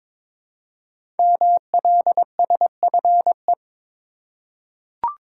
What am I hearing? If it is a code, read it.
Morse code, heard as a keyed tone, about 22 wpm, 700 Hz. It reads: MLSFE
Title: Code Groups - 5 Letters - 22wpm